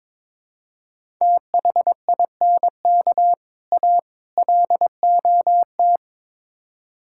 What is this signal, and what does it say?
Morse code, heard as a keyed tone, about 22 wpm, 700 Hz. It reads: THINK A LOT
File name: practice / 22wpm / Build A Phrase - 5-Word Phrases - 22wpm